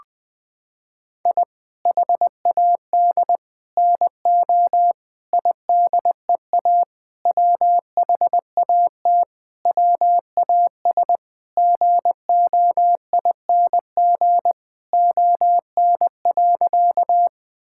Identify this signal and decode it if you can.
Morse code, heard as a keyed tone, about 20 wpm, 700 Hz. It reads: I HAD NO IDEA WHAT WAS GOING ON.
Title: Sentences with 8-Word Phrases - 20wpm